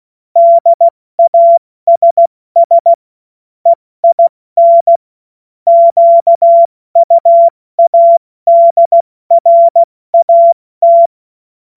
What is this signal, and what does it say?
Morse code, heard as a keyed tone, about 16 wpm, spaced as for 14 wpm, 680 Hz. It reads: DASS EIN QUADRAT
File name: Morsekurs / Witze / Witz 21. Juni 2022 70 F80